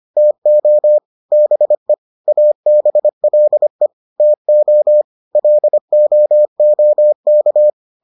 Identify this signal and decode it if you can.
Morse code, heard as a keyed tone, about 25 wpm, 600 Hz. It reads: TO BE ABLE TO LOOK